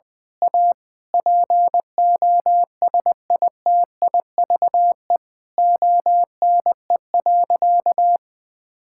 Morse code, heard as a keyed tone, about 21 wpm, 700 Hz. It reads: A POSITIVE ONE.
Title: Sentences from Top 1000 Words - Review Part 6 - 20wpm